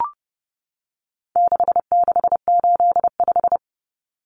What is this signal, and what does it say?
Morse code, heard as a keyed tone, about 30 wpm, 700 Hz. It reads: 6685
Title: Common Cut Numbers - 4-Digit 30wpm